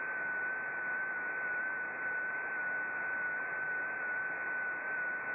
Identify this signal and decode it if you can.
FT8.
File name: contests / sept-vhf-k5tr- -0300